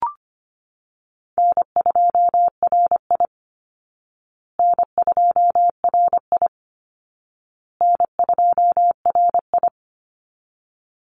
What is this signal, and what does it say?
Morse code, heard as a keyed tone, about 25 wpm, 700 Hz. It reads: N2RI N2RI N2RI
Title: Call Signs Repeated 2x - 25wpm